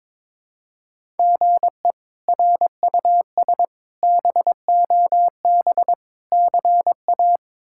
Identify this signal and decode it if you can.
Morse code, heard as a keyed tone, about 22 wpm, 700 Hz. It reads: GE RUS BOB CA
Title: K1USN SST QSO Practice - 22wpm